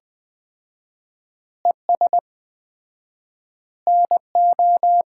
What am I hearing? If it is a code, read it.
Morse code, heard as a keyed tone, about 20 wpm, 700 Hz. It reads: ES NO
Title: Common QSO Phrases - 4x Word-Spacing - 20wpm